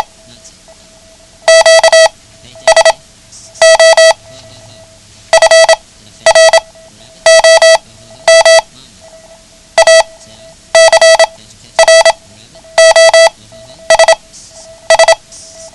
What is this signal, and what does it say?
Morse code, heard as a keyed tone, about 27 wpm, spaced as for 11 wpm, 650 Hz. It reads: QSO FROM ACROSS
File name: ham radio / CodeQuick / Code Quick 09